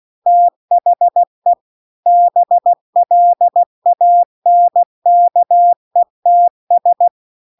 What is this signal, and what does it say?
Morse code, heard as a keyed tone, about 16 wpm, 700 Hz. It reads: THE BLANKETS